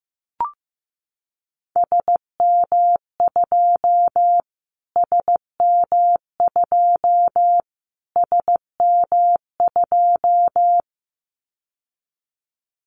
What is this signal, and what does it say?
Morse code, heard as a keyed tone, about 15 wpm, 700 Hz. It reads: SM2 SM2 SM2
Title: Callsign Prefixes for Top 25 DXCC - Repeated 3x - 15wpm